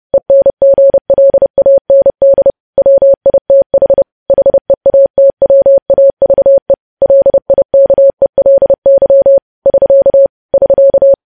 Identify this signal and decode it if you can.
Morse code, heard as a keyed tone, about 30 wpm, 550 Hz. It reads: ENGLAND WITH HEATWAVE LIKELY <SK> <SK>
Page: news headlines in morse code at 30 wpm